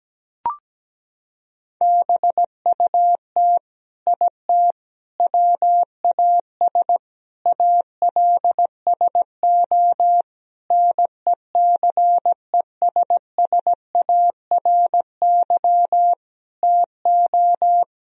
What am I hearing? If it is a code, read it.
Morse code, heard as a keyed tone, about 17 wpm, 700 Hz. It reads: BUT IT WAS ALSO NECESSARY TO
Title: Sentences from Top 1000 Words - Review Part 2 - 17wpm